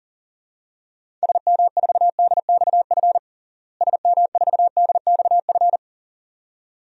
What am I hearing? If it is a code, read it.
Morse code, heard as a keyed tone, about 40 wpm, 700 Hz. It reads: SM4DXF SM4DXF
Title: DXCC-100 Call Signs - 6 Characters - Repeated 2x - 40wpm